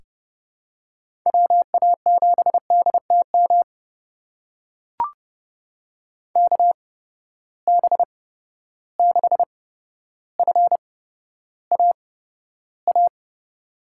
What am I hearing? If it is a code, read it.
Morse code, heard as a keyed tone, about 31 wpm, spaced as for 13 wpm, 700 Hz. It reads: WA7DTM KB6FAA
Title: USA Call Signs - 6 Characters - 8x Character Spacing - 30wpm